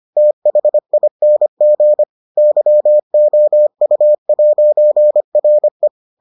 Morse code, heard as a keyed tone, about 25 wpm, 600 Hz. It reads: THING YOU'RE